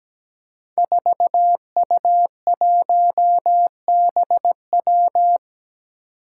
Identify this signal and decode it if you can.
Morse code, heard as a keyed tone, about 17 wpm, 700 Hz. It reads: 4U1BW
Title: Five Challenging Letter-Number Combinations 17wpm